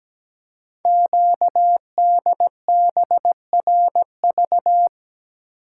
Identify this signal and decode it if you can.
Morse code, heard as a keyed tone, about 17 wpm, 700 Hz. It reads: QDBRV
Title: Code Groups - 5 Letters - 17wpm